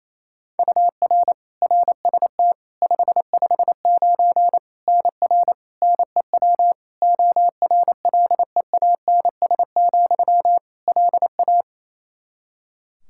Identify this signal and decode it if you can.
Morse code, heard as a keyed tone, about 28 wpm, 700 Hz. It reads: UR RST 559 NR NEW ORLEANS, LA
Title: K1USN SST QSO Practice 28wpm